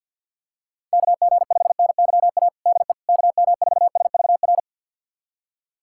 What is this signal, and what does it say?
Morse code, heard as a keyed tone, about 52 wpm, 690 Hz. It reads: KG5NYA DE KM4SVR
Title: USA Callsign DE Callsign - 6 Characters - 50wpm